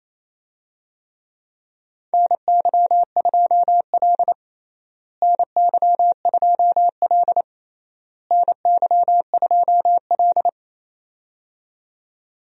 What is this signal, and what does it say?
Morse code, heard as a keyed tone, about 28 wpm, 700 Hz. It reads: NY2L NY2L NY2L
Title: USA Call Signs - 4 Characters - Repeated 3x - 28wpm